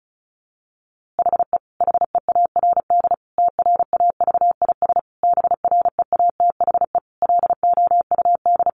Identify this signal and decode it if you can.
Morse code, heard as a keyed tone, about 34 wpm, 700 Hz. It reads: HE HEARD TRAVIS BREATHE LOUD
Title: A Sound Of Thunder 35wpm